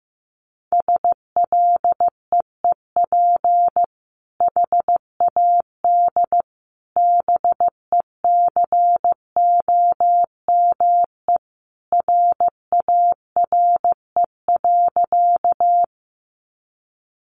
Morse code, heard as a keyed tone, about 15 wpm, 700 Hz. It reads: SLEEP HAD BECOME RARE.
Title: Sentences from Top 1100 Words 15wpm